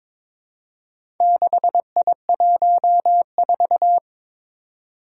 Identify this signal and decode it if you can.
Morse code, heard as a keyed tone, about 23 wpm, 700 Hz. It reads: 6I14
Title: Four Challenging Letter-Number Combinations 22wpm